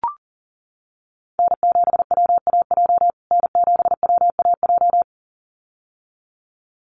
Morse code, heard as a keyed tone, about 40 wpm, 700 Hz. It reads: N7WAJ N7WAJ
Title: USA Call Signs Repeated 2x 40wpm